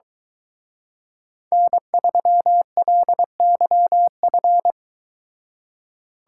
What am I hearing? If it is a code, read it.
Morse code, heard as a keyed tone, about 22 wpm, 700 Hz. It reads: N3LYF